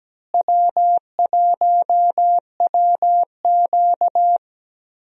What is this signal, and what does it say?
Morse code, heard as a keyed tone, about 17 wpm, 700 Hz. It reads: W1WQ
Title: USA Call Signs - 4 Characters - Repeated 2x - 17wpm